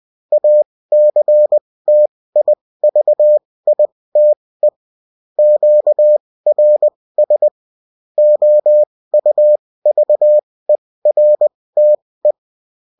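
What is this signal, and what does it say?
Morse code, heard as a keyed tone, about 20 wpm, spaced as for 16 wpm, 600 Hz. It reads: ACTIVITE QRS OUVERTE
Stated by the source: Morse code bulletin of a French amateur CW union, sent as text and in CW.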